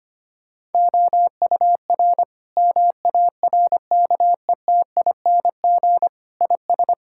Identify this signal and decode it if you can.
Morse code, heard as a keyed tone, about 25 wpm, 700 Hz. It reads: OUR MARKETING IS
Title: Sentences from Top 1000 Words - Review Part 5 - 25wpm